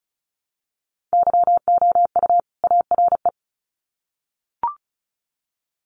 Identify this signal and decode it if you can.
Morse code, heard as a keyed tone, about 34 wpm, 700 Hz. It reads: YOU ARE
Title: Sets of 2 Words 35wpm